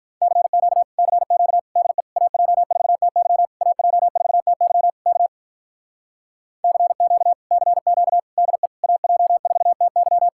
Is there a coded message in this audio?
Yes: Morse code from 0.2 to 10.3 s, heard as a keyed tone, about 48 wpm, 700 Hz. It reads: CQ CQ DE AJ4TY AJ4TY K CQ CQ DE AJ4TY